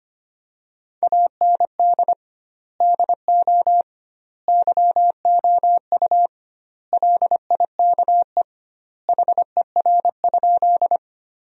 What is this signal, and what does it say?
Morse code, heard as a keyed tone, about 25 wpm, 700 Hz. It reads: AND DO YOU LIKE HER?
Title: 2x Word-Spacing - Sentences from Top 100 Words - 25wpm